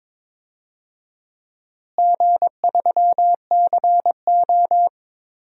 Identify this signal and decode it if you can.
Morse code, heard as a keyed tone, about 22 wpm, 700 Hz. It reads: G3CO
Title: DXCC-100 Call Signs - 4 Characters - Repeated 2x - 22wpm